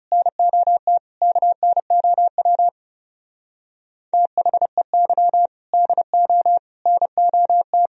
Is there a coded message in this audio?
Yes: Morse code from 0.1 to 7.9 s, heard as a keyed tone, about 39 wpm, 700 Hz. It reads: NOT KNOW THEY DO NOT